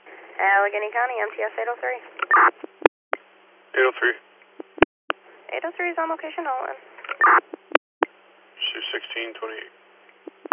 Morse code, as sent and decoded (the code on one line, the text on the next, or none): none